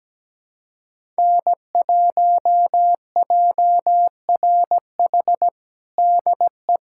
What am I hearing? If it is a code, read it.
Morse code, heard as a keyed tone, about 17 wpm, 700 Hz. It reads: N1JRH DE
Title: USA Callsign DE Callsign - 5 Characters - 17wpm